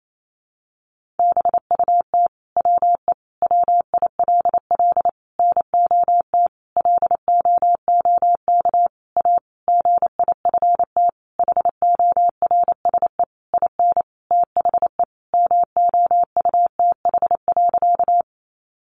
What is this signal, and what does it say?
Morse code, heard as a keyed tone, about 28 wpm, 700 Hz. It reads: BUT WE WILL NOT LOOK A GIFT HORSE IN THE MOUTH.
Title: Sentences from Top 1500 Words 28wpm